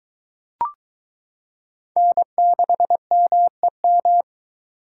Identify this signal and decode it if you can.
Morse code, heard as a keyed tone, about 21 wpm, 700 Hz. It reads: N6MEM